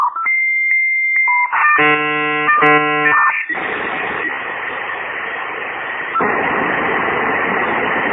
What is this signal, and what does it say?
Dial-up modem handshake.